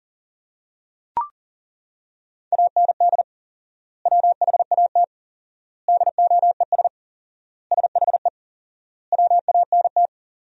Morse code, heard as a keyed tone, about 40 wpm, 700 Hz. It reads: AND WHAT DOES SHE WANT
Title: 4x Word-Spacing - Sentences from Top 400 Words - 40wpm